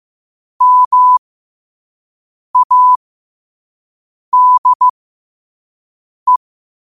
Morse code, heard as a keyed tone, about 15 wpm, spaced as for 5 wpm, 1000 Hz. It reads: MADE
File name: qotdfiles / 5wpm